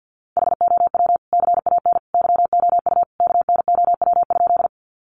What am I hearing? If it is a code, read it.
Morse code, heard as a keyed tone, about 50 wpm, 700 Hz. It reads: HOW CAN YOU KNOW?
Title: Sentences from Top 100 Words 50wpm